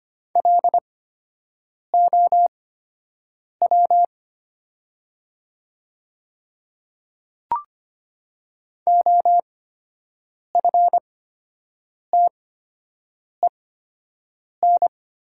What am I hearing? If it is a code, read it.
Morse code, heard as a keyed tone, about 25 wpm, spaced as for 7 wpm, 700 Hz. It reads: LOW OFTEN
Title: 8x Character Spacing - Top 500 Words - 25wpm